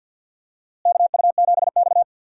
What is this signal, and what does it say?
Morse code, heard as a keyed tone, about 52 wpm, 700 Hz. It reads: KU7X
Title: USA Call Signs - 4 Characters - Repeated 2x - 50wpm